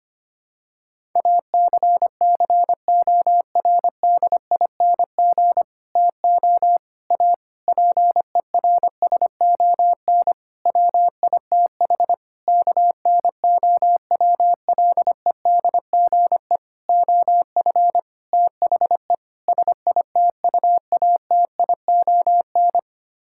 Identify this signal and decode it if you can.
Morse code, heard as a keyed tone, about 25 wpm, 700 Hz. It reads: ACCORDING TO A PERSON WITH KNOWLEDGE OF THE SITUATION
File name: Sets of 9 Words - 25wpm